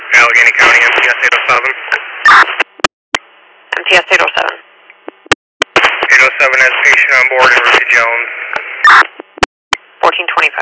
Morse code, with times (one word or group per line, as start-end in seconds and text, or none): none